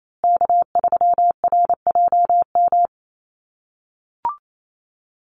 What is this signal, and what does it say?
Morse code, heard as a keyed tone, about 28 wpm, 700 Hz. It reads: K3RJM
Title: USA Call Signs Repeated 2x 28wpm